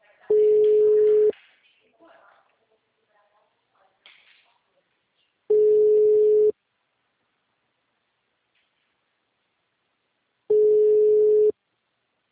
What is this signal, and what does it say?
Ringback tone (Europe, 425 Hz), with speech.
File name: out- -2029- - - .27346